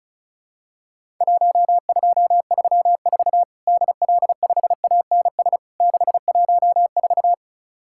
Morse code, heard as a keyed tone, about 34 wpm, 700 Hz. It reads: 1234 DL5ANS 614
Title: Worked All Europe DX Contest - QTC Exchange - 35wpm